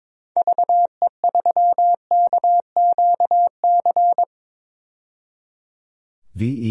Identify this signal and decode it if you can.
Morse code, heard as a keyed tone, about 21 wpm, 700 Hz. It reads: VE3KQC
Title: DXCC-100 Call Signs - 6 Characters - 22wpm